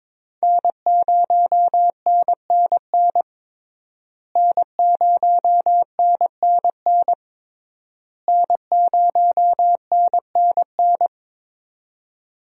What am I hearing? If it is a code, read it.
Morse code, heard as a keyed tone, about 21 wpm, 700 Hz. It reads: N0NNN N0NNN N0NNN
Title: USA Call Signs - 5 Characters - Repeated 3x - 22wpm